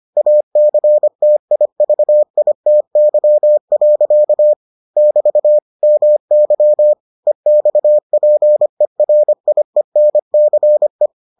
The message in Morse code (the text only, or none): ACTIVITY. <BT> MY EXPERIENCE